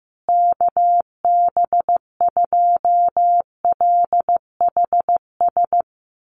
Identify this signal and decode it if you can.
Morse code, heard as a keyed tone, about 15 wpm, 700 Hz. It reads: KB2LHS
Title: USA Call Signs Repeated 3x 15wpm